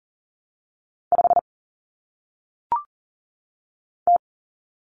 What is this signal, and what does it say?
Morse code, heard as a keyed tone, about 38 wpm, 700 Hz. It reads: 5 T
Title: Common Cut Numbers - 1-Digit 40wpm